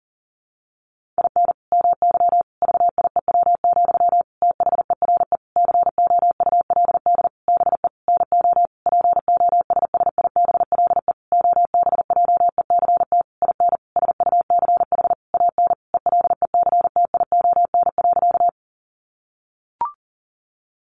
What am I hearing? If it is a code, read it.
Morse code, heard as a keyed tone, about 40 wpm, 700 Hz. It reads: IN MY VIEW, THERE COULD BE NO POSSIBLE OBJECT IN SUCH AN ELECTION.
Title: Sentences from Top 1500 Words 40wpm